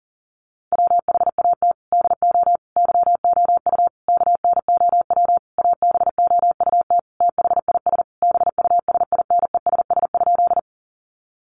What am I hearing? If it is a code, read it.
Morse code, heard as a keyed tone, about 40 wpm, 700 Hz. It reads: WHAT DO YOU KNOW ABOUT THIS BUSINESS?
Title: Sentences from Top 600 Words 40wpm